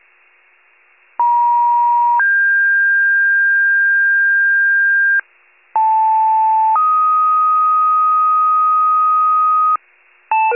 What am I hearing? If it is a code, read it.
Two-tone page, heard as a one-second tone, then a three-second tone, about 950 then 1600 Hz, about 880 then 1250 Hz, with noise. It is heard from 1.2 to 9.8 s.
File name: pager / 76 FIRE / wcalert